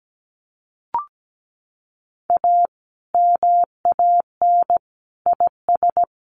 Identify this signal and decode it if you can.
Morse code, heard as a keyed tone, about 17 wpm, 700 Hz. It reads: A MAN IS
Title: Sentences from Top 1400 Words 17wpm